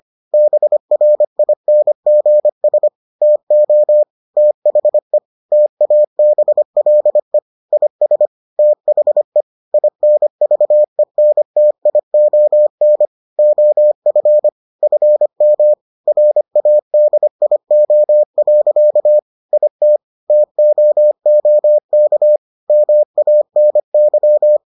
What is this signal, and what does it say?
Morse code, heard as a keyed tone, about 25 wpm, 600 Hz. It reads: BRINGS TO THE TABLE IS THE INVENTION OF FM RADIO. IT TOOK MANY